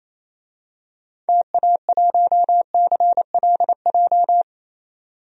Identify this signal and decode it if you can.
Morse code, heard as a keyed tone, about 27 wpm, 700 Hz. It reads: TA1CLJ
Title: DXCC-100 Call Signs - 6 Characters - 28wpm